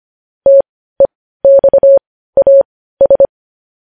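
Morse code, heard as a keyed tone, about 25 wpm, spaced as for 15 wpm, 550 Hz. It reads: TEXAS
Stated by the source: Morse code news headlines broadcast.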